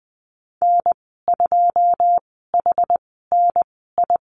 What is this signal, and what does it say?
Morse code, heard as a keyed tone, about 20 wpm, spaced as for 15 wpm, 700 Hz. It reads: N2HNI
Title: 088 Lesson 15g - Number 5 - Call Signs 20wpm